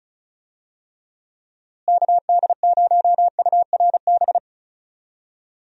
Morse code, heard as a keyed tone, about 34 wpm, 700 Hz. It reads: KD0URB
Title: USA Call Signs - 6 Characters - Repeated 3x - 35wpm